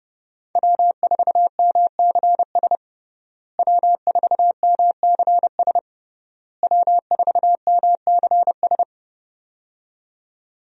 Morse code, heard as a keyed tone, about 32 wpm, 700 Hz. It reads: W4MCS W4MCS W4MCS
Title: USA Call Signs - 5 Characters - Repeated 3x - 30wpm